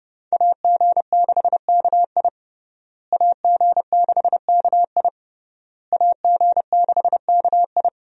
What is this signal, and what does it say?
Morse code, heard as a keyed tone, about 30 wpm, 700 Hz. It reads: AG6KI AG6KI AG6KI
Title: USA Call Signs - 5 Characters - Repeated 3x - 30wpm